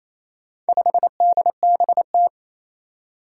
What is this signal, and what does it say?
Morse code, heard as a keyed tone, about 27 wpm, 700 Hz. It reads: 5DBT
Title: Four Challenging Letter-Number Combinations 28wpm